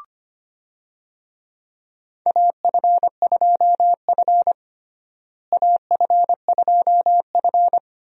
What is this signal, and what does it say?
Morse code, heard as a keyed tone, about 25 wpm, 700 Hz. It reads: AF2F AF2F
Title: USA Call Signs - 4 Characters - Repeated 3x - 25wpm